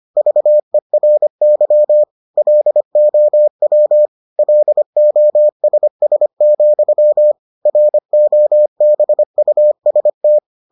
Morse code, heard as a keyed tone, about 25 wpm, 600 Hz. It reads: VERY LOW LOSS, ROBUST